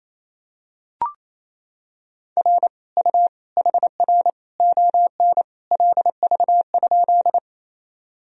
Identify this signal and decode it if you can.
Morse code, heard as a keyed tone, about 28 wpm, 700 Hz. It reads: R U HR ON LV?
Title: Sentences from Top 400 Words and Abbreviations 28wpm